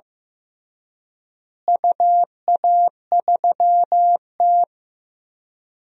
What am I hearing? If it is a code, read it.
Morse code, heard as a keyed tone, about 15 wpm, 700 Hz. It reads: UA3T
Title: DXCC-100 Call Signs - 4 Characters - Repeated 3x - 15wpm